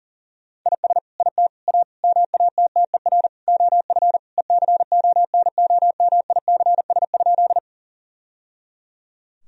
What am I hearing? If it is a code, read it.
Morse code, heard as a keyed tone, about 37 wpm, 700 Hz. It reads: IS IT A MATTER OF ECONOMICS?
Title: Sentences from Top 1000 Words - Review Part 3 - 40wpm